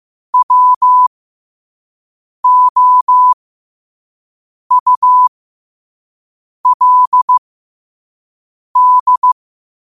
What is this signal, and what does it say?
Morse code, heard as a keyed tone, about 15 wpm, spaced as for 5 wpm, 1000 Hz. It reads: WOULD
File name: qotdfiles / 5wpm